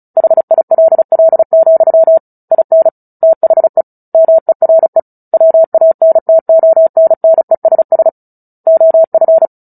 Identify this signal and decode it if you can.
Morse code, heard as a keyed tone, about 34 wpm, 650 Hz. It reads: HILL, IN THE MERE WANTONNESS OF
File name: Book Two / 09-Chapter 9 - 35 WPM